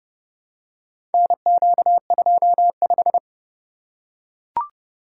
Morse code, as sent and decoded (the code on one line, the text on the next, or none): -. --.- ..--- .....
NQ25